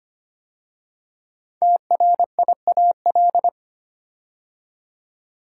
Morse code, heard as a keyed tone, about 25 wpm, 700 Hz. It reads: TRIAL
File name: 5 Letter Words - First 3 Letters Given - 25wpm